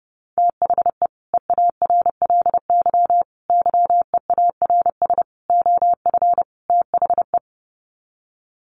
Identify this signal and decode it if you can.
Morse code, heard as a keyed tone, about 28 wpm, 700 Hz. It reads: THE EARLY YEARS OF THE
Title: Sets of 5 Words Encore 30wpm